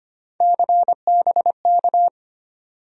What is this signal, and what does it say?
Morse code, heard as a keyed tone, about 25 wpm, 700 Hz. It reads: CBK
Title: USA Call Sign Postfixes - 3 Letters 25wpm